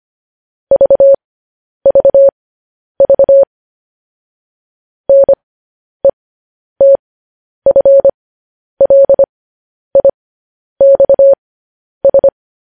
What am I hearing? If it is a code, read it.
Morse code, heard as a keyed tone, about 25 wpm, spaced as for 10 wpm, 550 Hz. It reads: VVV NETFLIXS
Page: news-headlines-in-morse-code-at-10-wpm